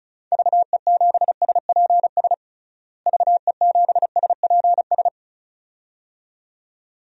Morse code, heard as a keyed tone, about 36 wpm, 700 Hz. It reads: VE7SPS VE7SPS
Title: DXCC-100 Call Signs - 6 Characters - Repeated 2x - 35wpm